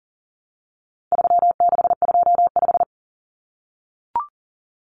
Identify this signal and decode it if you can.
Morse code, heard as a keyed tone, about 40 wpm, 700 Hz. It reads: 3625
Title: Common Cut Numbers - 4-Digit 40wpm